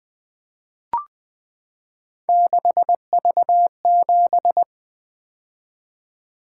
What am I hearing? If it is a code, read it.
Morse code, heard as a keyed tone, about 20 wpm, 700 Hz. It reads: 6V7
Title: Three Letter-Number Combinations - 20wpm